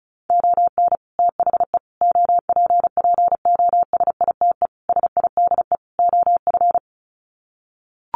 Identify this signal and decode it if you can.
Morse code, heard as a keyed tone, about 34 wpm, 700 Hz. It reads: ON THE OPPOSITE SIDE OF
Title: Sets of 5 Words Encore2 35wpm